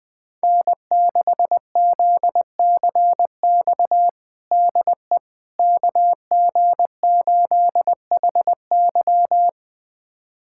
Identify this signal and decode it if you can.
Morse code, heard as a keyed tone, about 19 wpm, 700 Hz. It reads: N6ZCX DE KG8HY
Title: USA Callsign DE Callsign 20wpm